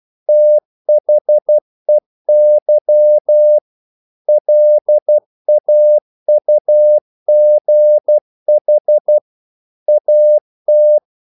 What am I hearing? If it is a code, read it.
Morse code, heard as a keyed tone, about 12 wpm, 600 Hz. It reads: THEY LAUGH AT
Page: English quotations in CW (telegraphy - morse)